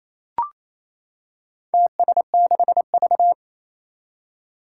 Morse code, heard as a keyed tone, about 28 wpm, 700 Hz. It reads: TS6V
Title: Four Challenging Letter-Number Combinations 28wpm